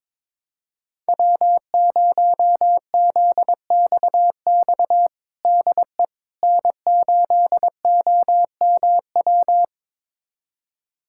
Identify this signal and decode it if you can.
Morse code, heard as a keyed tone, about 22 wpm, 700 Hz. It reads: W0ZXX DE N8OMW
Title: USA Callsign DE Callsign - 5 Characters - 22wpm